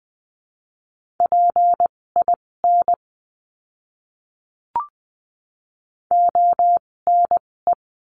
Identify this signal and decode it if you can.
Morse code, heard as a keyed tone, about 20 wpm, spaced as for 16 wpm, 700 Hz. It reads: PIN ONE
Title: 256 Lesson 42a - Words 12-20wpm